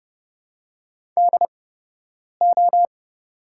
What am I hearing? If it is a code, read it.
Morse code, heard as a keyed tone, about 31 wpm, spaced as for 8 wpm, 700 Hz. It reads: DO
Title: 8x Character Spacing - Top 200 Words - 30wpm